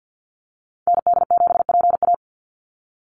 Morse code, heard as a keyed tone, about 48 wpm, 700 Hz. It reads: ND7PA
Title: NA SOTA Activator Callsigns 50wpm